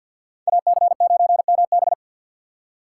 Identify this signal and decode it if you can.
Morse code, heard as a keyed tone, about 64 wpm, 700 Hz. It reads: AC9MB